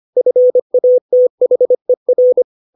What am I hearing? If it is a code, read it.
Morse code, heard as a keyed tone, about 25 wpm, 500 Hz. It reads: FATHER
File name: 632 words / 632 Words 25WPM